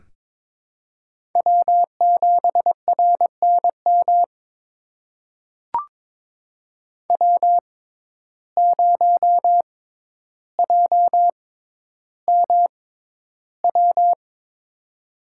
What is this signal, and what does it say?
Morse code, heard as a keyed tone, about 22 wpm, spaced as for 11 wpm, 700 Hz. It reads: W7RNM W0JMW
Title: USA Call Signs - 5 Characters - 6x Character Spacing - 22wpm